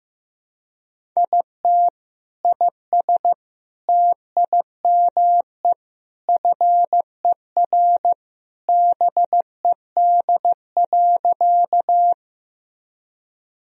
Morse code, heard as a keyed tone, about 15 wpm, 700 Hz. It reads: IT IS TIME FER BED.
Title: Sentences from Top 500 Words and Abbreviations 15wpm